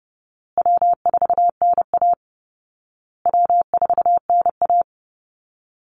Morse code, heard as a keyed tone, about 28 wpm, 700 Hz. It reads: W4NA W4NA
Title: Call Signs Repeated 2x - 30wpm